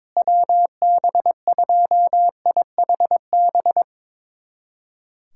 Morse code, heard as a keyed tone, about 22 wpm, 700 Hz. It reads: WB2IHB
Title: USA Callsign DE Callsign 22wpm